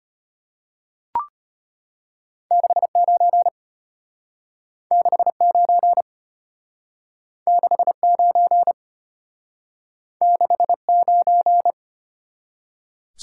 Morse code, heard as a keyed tone, about 33 wpm, 700 Hz. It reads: 69 69 69 69